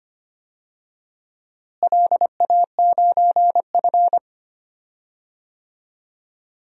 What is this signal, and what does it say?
Morse code, heard as a keyed tone, about 25 wpm, 700 Hz. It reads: LA9F
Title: DXCC-100 Call Signs - 4 Characters - Repeated 2x - 25wpm